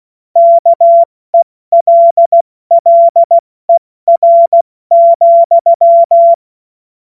Morse code, heard as a keyed tone, about 16 wpm, 680 Hz. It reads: KELLER,
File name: Witz-16-September-2025-70-F80